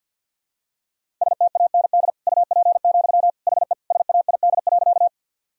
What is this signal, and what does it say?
Morse code, heard as a keyed tone, about 45 wpm, 700 Hz. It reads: STAND UP, HE SAID.
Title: Sentences from Top 1000 Words - Review Part 1 - 50wpm